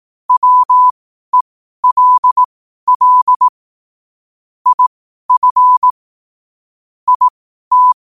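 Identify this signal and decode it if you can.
Morse code, heard as a keyed tone, about 18 wpm, spaced as for 12 wpm, 1000 Hz. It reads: WELL IF IT
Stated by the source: Morse practice sent at 12 wpm.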